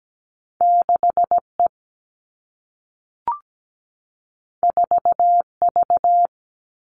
Morse code, heard as a keyed tone, about 17 wpm, 700 Hz. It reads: 6E 4V
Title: All Cut Numbers - 2-Digit 17wpm